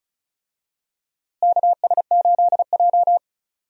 Morse code, heard as a keyed tone, about 34 wpm, 700 Hz. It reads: KS8J
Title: USA Call Signs - 4 Characters - Repeated 3x - 35wpm